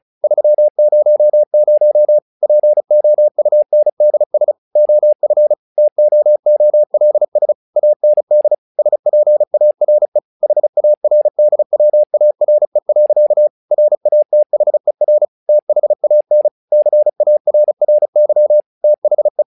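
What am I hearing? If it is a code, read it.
Morse code, heard as a keyed tone, about 34 wpm, 600 Hz. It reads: 300 POUNDS OF TOOLS AND SPARE HARDWARE. RATHER THAN CARRY THE
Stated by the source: Morse practice text "Tower Trouble" sent at 35 WPM, from the SKCC - Straight Key Century Club learning center audio files.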